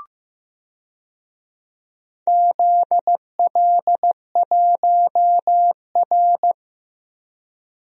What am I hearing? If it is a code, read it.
Morse code, heard as a keyed tone, about 15 wpm, 700 Hz. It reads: ZL1R
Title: DXCC-100 Call Signs - 4 Characters - 15wpm